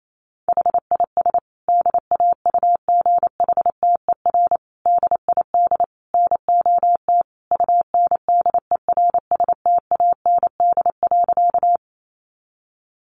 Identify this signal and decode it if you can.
Morse code, heard as a keyed tone, about 28 wpm, 700 Hz. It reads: HIS DAUGHTER DID NOT UNDERSTAND.
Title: Sentences from Top 1500 Words 28wpm